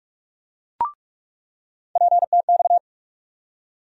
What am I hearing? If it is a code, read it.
Morse code, heard as a keyed tone, about 42 wpm, 700 Hz. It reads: PTX
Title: USA Call Sign Postfixes - 3 Letters 45wpm